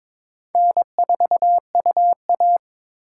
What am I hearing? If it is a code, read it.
Morse code, heard as a keyed tone, about 21 wpm, 700 Hz. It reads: N4UA
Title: USA Call Signs - 4 Characters - 3x Character Spacing - 22wpm